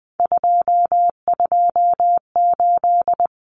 Morse code, heard as a keyed tone, about 20 wpm, 700 Hz. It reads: 228
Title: Common Cut Numbers - 3-Digit 20wpm